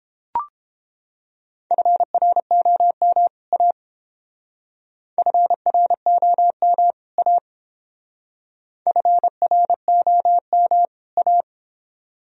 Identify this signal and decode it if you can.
Morse code, heard as a keyed tone, about 28 wpm, 700 Hz. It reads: FROM A FROM A FROM A